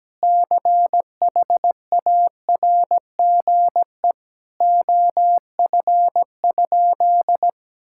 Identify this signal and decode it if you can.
Morse code, heard as a keyed tone, about 17 wpm, 700 Hz. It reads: CHARGE OF?
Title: Sentences from Top 1000 Words - Review Part 6 - 17wpm